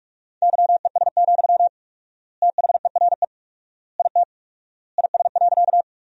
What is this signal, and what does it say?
Morse code, heard as a keyed tone, about 42 wpm, 700 Hz. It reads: YES, THERE IT IS.
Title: 4x Word-Spacing - Sentences from Top 500 Words - 45wpm